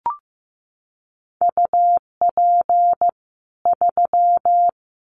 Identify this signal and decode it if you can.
Morse code, heard as a keyed tone, about 15 wpm, 700 Hz. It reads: UP 3
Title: Common QSO Phrases 15wpm